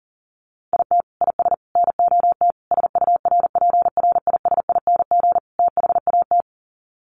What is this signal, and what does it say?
Morse code, heard as a keyed tone, about 40 wpm, 700 Hz. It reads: IT IS NOT SURPRISING THAT
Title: Sets of 5 Words 40wpm